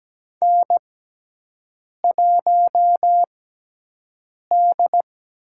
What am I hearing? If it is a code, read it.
Morse code, heard as a keyed tone, about 17 wpm, spaced as for 6 wpm, 700 Hz. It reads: N1D
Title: USA Call Signs - 5 Characters - 6x Character Spacing - 17wpm